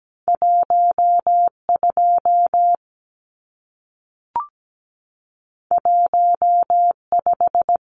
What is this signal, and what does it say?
Morse code, heard as a keyed tone, about 17 wpm, 700 Hz. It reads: 12 15
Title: Numbers - 2-Digit 17wpm